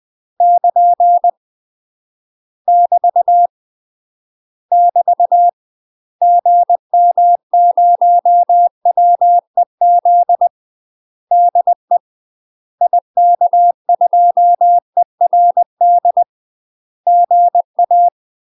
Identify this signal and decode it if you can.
Morse code, heard as a keyed tone, about 20 wpm, 700 Hz. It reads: <KN> <BT> <BT> GM0WEZ DE IK2ERD GA
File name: UK MASTER QSOs TRAINING / 20WPM / UK MASTER QSOs 0045